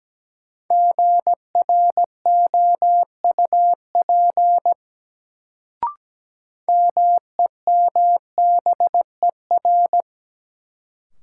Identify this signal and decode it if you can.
Morse code, heard as a keyed tone, about 17 wpm, 700 Hz. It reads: GROUP MEMBER
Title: Sets of 3 Words - New Word Drill - 17wpm